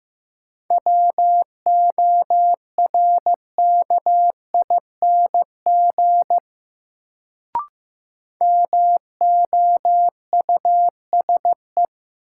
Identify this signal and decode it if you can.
Morse code, heard as a keyed tone, about 15 wpm, 700 Hz. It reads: WORKING MOUSE
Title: Sentences from Top 700 Words - New Word Drill - 15wpm